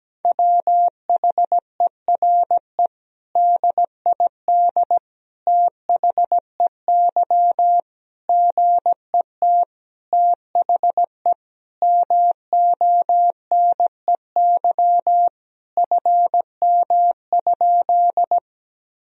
Morse code, heard as a keyed tone, about 17 wpm, 700 Hz. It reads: WHERE DID THEY GET THE MONEY FM?
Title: Sentences from Top 400 Words and Abbreviations 17wpm